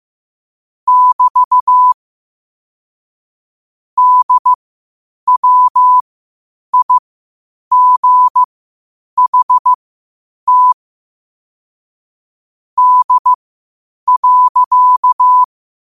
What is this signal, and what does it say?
Morse code, heard as a keyed tone, about 15 wpm, spaced as for 8 wpm, 1000 Hz. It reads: <BT> DWIGHT D.
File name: qotdfiles / 8wpm